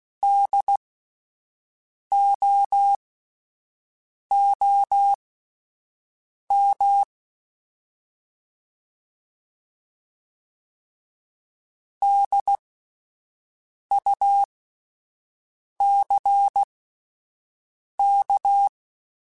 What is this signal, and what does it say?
Morse code, heard as a keyed tone, about 16 wpm, spaced as for 5 wpm, 800 Hz. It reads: DOOM DUCK
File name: Spaced Alphas / Spaced Alphas mp3s / Ds sp @10